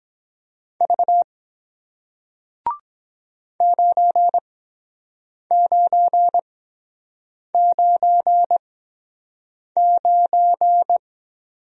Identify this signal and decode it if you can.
Morse code, heard as a keyed tone, about 23 wpm, spaced as for 7 wpm, 700 Hz. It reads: V 9999